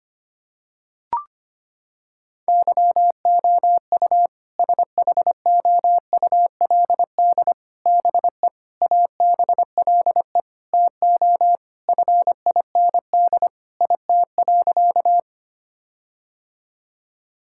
Morse code, heard as a keyed tone, about 25 wpm, 700 Hz. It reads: YOU SHOULD BE ABLE TO FIND IT.
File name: practice / 25wpm / Sentences with 4-Word Phrases - 25wpm